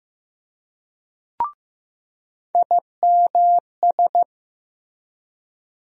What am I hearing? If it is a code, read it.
Morse code, heard as a keyed tone, about 15 wpm, 700 Hz. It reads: IMS
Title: USA Call Sign Postfixes - 3 Letters 15wpm